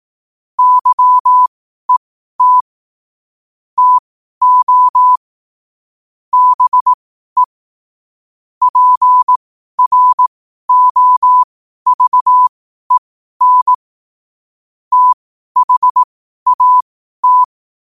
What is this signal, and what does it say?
Morse code, heard as a keyed tone, about 18 wpm, spaced as for 12 wpm, 1000 Hz. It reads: YET TO BE PROVEN THAT
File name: qotdfiles / 12wpm